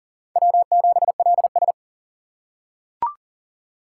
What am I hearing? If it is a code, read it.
Morse code, heard as a keyed tone, about 37 wpm, 700 Hz. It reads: W7LS
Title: USA Call Signs - 4 Characters - 4x Character Spacing - 40wpm